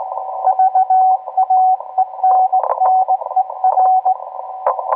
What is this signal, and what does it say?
Morse code, heard as a keyed tone, about 18 wpm, 750 Hz. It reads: VAENNER